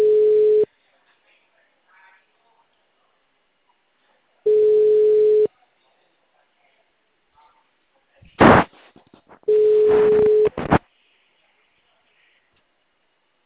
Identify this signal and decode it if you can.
Ringback tone (Europe, 425 Hz).